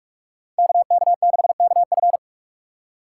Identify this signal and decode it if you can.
Morse code, heard as a keyed tone, about 46 wpm, 700 Hz. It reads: KK6KF
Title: USA Call Signs - 5 Characters - 3x Character Spacing - 45wpm